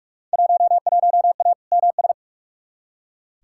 Morse code, heard as a keyed tone, about 45 wpm, 700 Hz. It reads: 11A MS
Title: Field Day Exchanges Encore - 45wpm